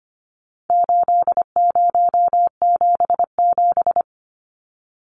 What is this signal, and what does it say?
Morse code, heard as a keyed tone, about 25 wpm, 700 Hz. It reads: 8077
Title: Numbers - 4-Digit 25wpm